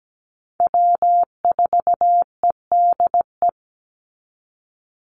Morse code, heard as a keyed tone, about 17 wpm, 700 Hz. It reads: W4EDE
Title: Call Signs - 17wpm